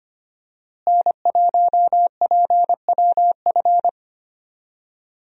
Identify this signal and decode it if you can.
Morse code, heard as a keyed tone, about 25 wpm, 700 Hz. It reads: N1PWF
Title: USA Call Signs - 5 Characters - Repeated 3x - 25wpm